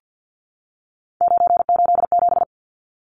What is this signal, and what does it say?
Morse code, heard as a keyed tone, about 50 wpm, 700 Hz. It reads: 987
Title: Numbers - 3-Digit 50wpm